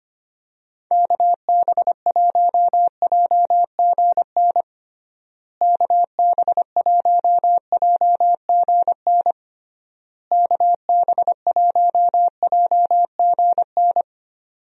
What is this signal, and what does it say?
Morse code, heard as a keyed tone, about 25 wpm, 700 Hz. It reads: KB1JGN KB1JGN KB1JGN
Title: USA Call Signs - 6 Characters - Repeated 3x - 25wpm